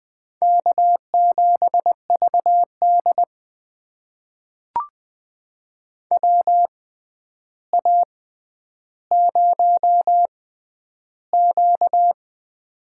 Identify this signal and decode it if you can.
Morse code, heard as a keyed tone, about 21 wpm, spaced as for 10 wpm, 700 Hz. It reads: K7VD WA0Q
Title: USA Call Signs - 4 Characters - 6x Character Spacing - 20wpm